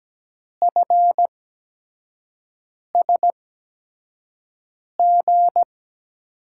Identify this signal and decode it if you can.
Morse code, heard as a keyed tone, about 17 wpm, spaced as for 5 wpm, 700 Hz. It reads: FSG
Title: USA Call Signs - 8x Character Spacing - 17wpm